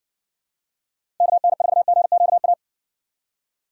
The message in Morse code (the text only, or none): KN4GQA